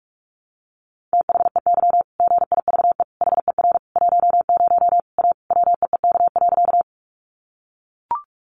Morse code, heard as a keyed tone, about 44 wpm, 700 Hz. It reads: THEY GIVE HER 10 A WEEK.
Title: Easy Phrases 45wpm